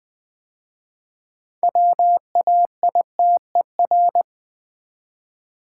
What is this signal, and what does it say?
Morse code, heard as a keyed tone, about 20 wpm, 700 Hz. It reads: WAITER
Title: 6 Letter Words - First 3 Letters Given - 20wpm